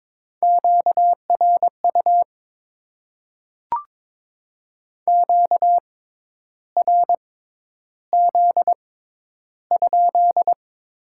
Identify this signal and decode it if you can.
Morse code, heard as a keyed tone, about 21 wpm, spaced as for 10 wpm, 700 Hz. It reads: QRU QRZ?
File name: Top 90 QSO Elements - 6x Character Spacing - 22wpm